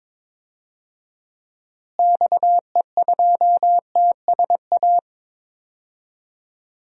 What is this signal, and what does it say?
Morse code, heard as a keyed tone, about 21 wpm, 700 Hz. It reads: XE2TSA
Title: DXCC-100 Call Signs - 6 Characters - Repeated 2x - 22wpm